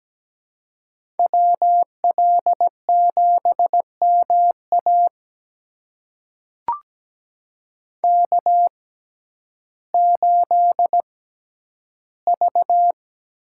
Morse code, heard as a keyed tone, about 17 wpm, spaced as for 10 wpm, 700 Hz. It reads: WL7MA K8V
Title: USA Call Signs - 5 Characters - 6x Character Spacing - 17wpm